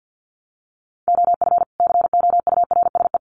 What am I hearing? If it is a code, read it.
Morse code, heard as a keyed tone, about 50 wpm, 700 Hz. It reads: OF COURSE
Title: Sets of 2 Words 50wpm